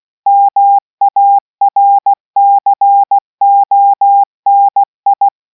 Morse code, heard as a keyed tone, about 16 wpm, 800 Hz. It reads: MARCONI